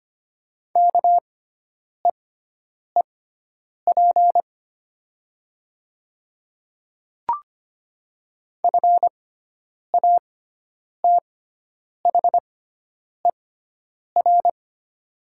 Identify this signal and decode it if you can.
Morse code, heard as a keyed tone, about 25 wpm, spaced as for 9 wpm, 700 Hz. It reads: KEEP FATHER
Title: 6x Character Spacing - Top 300 Words - 25wpm